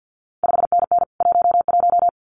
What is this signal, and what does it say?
Morse code, heard as a keyed tone, about 50 wpm, 700 Hz. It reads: 5NN 11
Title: CQ WW DX Contest Exchange - 50wpm